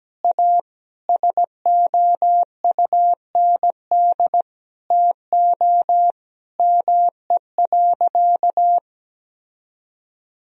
Morse code, heard as a keyed tone, about 17 wpm, 700 Hz. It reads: A SOUND TO ME.
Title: Sentences from Top 1000 Words - Review Part 1 - 17wpm